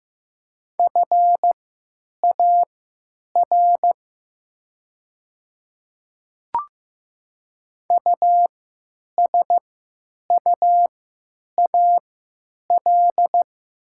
Morse code, heard as a keyed tone, about 15 wpm, spaced as for 9 wpm, 700 Hz. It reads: FAR USUAL
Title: 3x Character Spacing - Top 500 Words - 15wpm